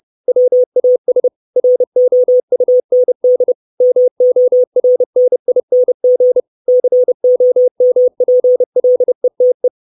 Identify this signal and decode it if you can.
Morse code, heard as a keyed tone, about 30 wpm, 500 Hz. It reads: WAS ROUND MORNING COMPLETE
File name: 632 Words 30WPM